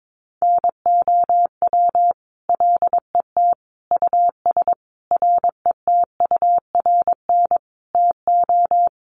Morse code, heard as a keyed tone, about 22 wpm, 700 Hz. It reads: NOW LET US RETURN TO
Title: Sentences from Top 800 Words 22wpm